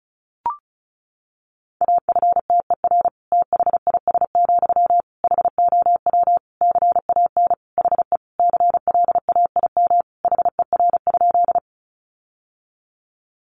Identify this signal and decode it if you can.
Morse code, heard as a keyed tone, about 34 wpm, 700 Hz. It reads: AFTER THIS, HOW CAN HE CLAIM HER?
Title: Sentences from Top 1500 Words 35wpm